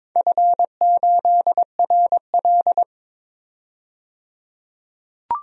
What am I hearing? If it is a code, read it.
Morse code, heard as a keyed tone, about 23 wpm, 700 Hz. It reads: F8RL
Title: DXCC-100 Call Signs - 4 Characters - 22wpm